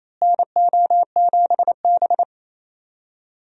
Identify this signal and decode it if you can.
Morse code, heard as a keyed tone, about 28 wpm, 700 Hz. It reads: NO7B
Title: USA Call Signs - 4 Characters - Repeated 2x - 28wpm